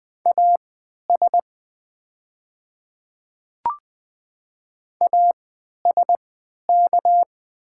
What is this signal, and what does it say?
Morse code, heard as a keyed tone, about 20 wpm, spaced as for 11 wpm, 690 Hz. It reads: AS ASK
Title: 3x Character Spacing - Top 300 Words - 20wpm